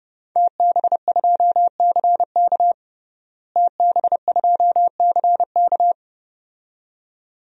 Morse code, heard as a keyed tone, about 30 wpm, 700 Hz. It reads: TB2CK TB2CK
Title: DXCC-100 Call Signs - 5 Characters - Repeated 2x - 30wpm